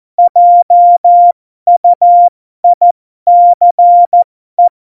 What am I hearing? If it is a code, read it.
Morse code, heard as a keyed tone, about 14 wpm, spaced as for 12 wpm, 700 Hz. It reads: JUICE
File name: Morsekurs / Witze